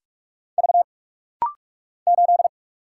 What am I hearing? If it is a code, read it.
Morse code, heard as a keyed tone, about 46 wpm, 700 Hz. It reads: V 8
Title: Single Letter-Number Alt - Rapid-Fire 45wpm